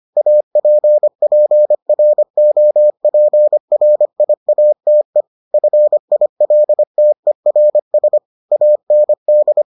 Morse code, heard as a keyed tone, about 25 wpm, 600 Hz. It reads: APPROPRIATE FILTERS AND